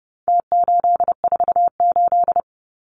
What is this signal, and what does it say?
Morse code, heard as a keyed tone, about 30 wpm, 700 Hz. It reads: T848
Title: Numbers - Contest Style - 4-Digit 30wpm